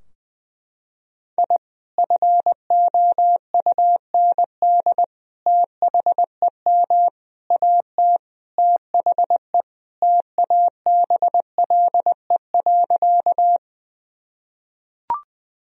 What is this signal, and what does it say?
Morse code, heard as a keyed tone, about 20 wpm, 700 Hz. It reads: I FOUND THEM AT THE TABLE.
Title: Sentences from Top 1000 Words - Review Part 2 - 20wpm